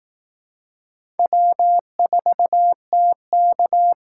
Morse code, heard as a keyed tone, about 19 wpm, 700 Hz. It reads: W4TK
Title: Speed Racing - USA 1x2 Call Signs 15wpm